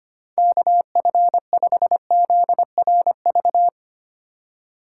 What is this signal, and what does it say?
Morse code, heard as a keyed tone, about 25 wpm, 700 Hz. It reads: KF5ZRV
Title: USA Call Signs - 3x Character Spacing - 25wpm